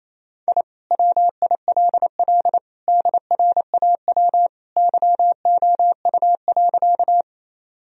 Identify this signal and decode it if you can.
Morse code, heard as a keyed tone, about 28 wpm, 700 Hz. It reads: I WILL DRAW YOU.
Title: Sentences from Top 1000 Words - Review Part 1 - 28wpm